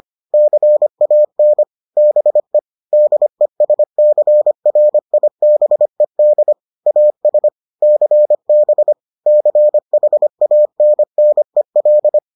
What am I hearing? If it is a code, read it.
Morse code, heard as a keyed tone, about 25 wpm, 600 Hz. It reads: CAN BE DESCRIBED AS CB CHANNEL